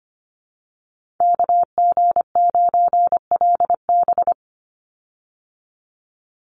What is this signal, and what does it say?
Morse code, heard as a keyed tone, about 25 wpm, 700 Hz. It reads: KG9LB
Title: Call Signs Repeated 3x - 25wpm